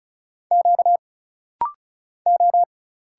Morse code, heard as a keyed tone, about 34 wpm, 700 Hz. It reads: Q O
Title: Single Letters - Rapid-Fire 35wpm